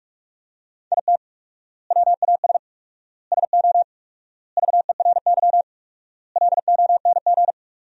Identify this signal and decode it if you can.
Morse code, heard as a keyed tone, about 44 wpm, 690 Hz. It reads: IT WAS SO VERY LONG
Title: 4x Word-Spacing - Sentences from Top 500 Words - 45wpm